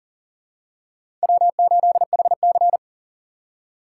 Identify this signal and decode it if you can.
Morse code, heard as a keyed tone, about 40 wpm, 700 Hz. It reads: W8HC
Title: USA Call Signs - 4 Characters - 6x Character Spacing - 40wpm